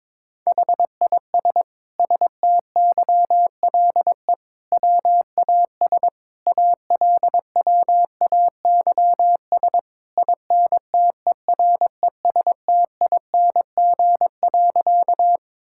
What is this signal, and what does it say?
Morse code, heard as a keyed tone, about 22 wpm, 700 Hz. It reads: HIS STYLE WAS ALWAYS INTERESTING.
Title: Sentences from Top 1000 Words - Review Part 5 - 22wpm